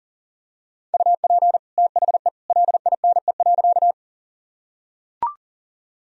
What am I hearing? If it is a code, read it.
Morse code, heard as a keyed tone, about 40 wpm, 700 Hz. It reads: UP THE LINE.
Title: Sentences from Top 1000 Words - Review Part 1 - 40wpm